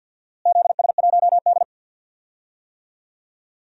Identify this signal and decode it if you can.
Morse code, heard as a keyed tone, about 52 wpm, 700 Hz. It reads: ZS1D
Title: DXCC-100 Call Signs - 4 Characters - Repeated 2x - 50wpm